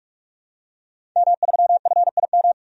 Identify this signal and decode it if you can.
Morse code, heard as a keyed tone, about 45 wpm, 700 Hz. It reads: M3FIM
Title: DXCC-100 Call Signs - 5 Characters - 45wpm